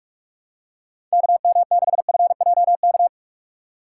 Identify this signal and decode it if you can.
Morse code, heard as a keyed tone, about 41 wpm, 700 Hz. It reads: KM6FJK
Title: USA Call Signs - 6 Characters - 2x Character Spacing - 45wpm